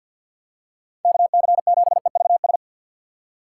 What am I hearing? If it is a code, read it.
Morse code, heard as a keyed tone, about 53 wpm, 700 Hz. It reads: KC7EVS
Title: USA Call Signs - 2x Character Spacing - 50wpm